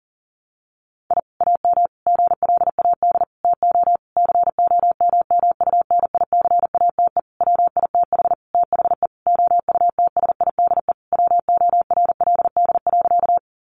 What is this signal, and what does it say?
Morse code, heard as a keyed tone, about 40 wpm, 700 Hz. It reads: I AM GLAD TO COMMUNICATE WITH THE OUTSIDE WORLD.
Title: Sentences from Top 1300 Words 40wpm